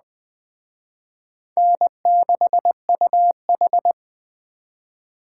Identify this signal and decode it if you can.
Morse code, heard as a keyed tone, about 20 wpm, 700 Hz. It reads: N6UH